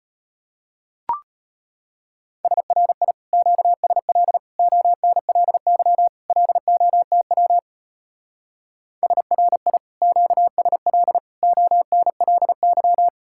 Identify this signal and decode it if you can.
Morse code, heard as a keyed tone, about 31 wpm, 700 Hz. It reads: SRI QSL ONLY LOTW SRI QSL ONLY